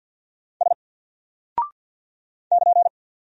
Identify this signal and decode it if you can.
Morse code, heard as a keyed tone, about 48 wpm, 700 Hz. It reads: S <KN>
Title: Single Character (Letter:Number:Punctuation:Prosign) Alt 50wpm